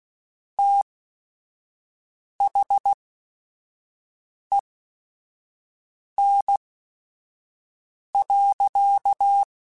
Morse code, heard as a keyed tone, about 16 wpm, spaced as for 5 wpm, 800 Hz. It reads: THEN.
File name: QSOs / QSOs-mp3s / WA7QOX 1@5